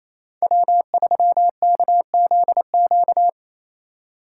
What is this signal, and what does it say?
Morse code, heard as a keyed tone, about 36 wpm, 700 Hz. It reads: W3KZQ